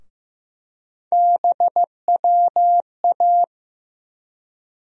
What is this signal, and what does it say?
Morse code, heard as a keyed tone, about 15 wpm, 700 Hz. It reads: BWA
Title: USA Call Sign Postfixes - 3 Letters 15wpm